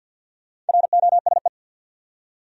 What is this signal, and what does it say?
Morse code, heard as a keyed tone, about 52 wpm, 700 Hz. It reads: ROSE
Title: USA Common Names - Max 5 Letters - 50wpm